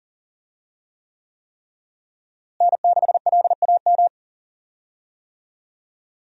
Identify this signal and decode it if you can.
Morse code, heard as a keyed tone, about 40 wpm, 700 Hz. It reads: N6LAM
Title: USA Call Signs - 5 Characters - Repeated 3x - 40wpm